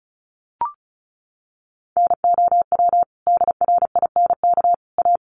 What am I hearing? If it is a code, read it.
Morse code, heard as a keyed tone, about 34 wpm, 700 Hz. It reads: NOW DRINK A
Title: Sentences from Top 1500 Words 35wpm